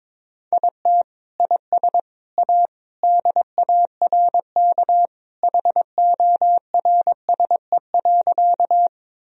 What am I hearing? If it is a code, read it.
Morse code, heard as a keyed tone, about 22 wpm, 700 Hz. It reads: IT IS A DARK HORSE.
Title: Sentences from Top 1000 Words - Review Part 1 - 22wpm